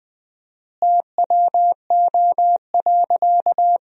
Morse code, heard as a keyed tone, about 20 wpm, 700 Hz. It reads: TWO.
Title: 4x Word-Spacing - Sentences from Top 300 Words - 20wpm